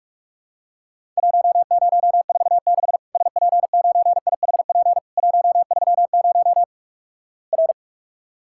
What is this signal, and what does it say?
Morse code, heard as a keyed tone, about 46 wpm, 700 Hz. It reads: 1046 SP9IHP 130 R
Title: Worked All Europe DX Contest - QTC Exchange - 45wpm